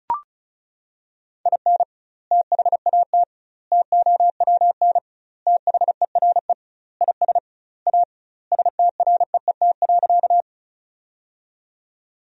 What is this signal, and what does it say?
Morse code, heard as a keyed tone, about 34 wpm, 700 Hz. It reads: IN THAT TOWN THERE IS A STREET.
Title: 2x Word-Spacing - Sentences from Top 500 Words - 35wpm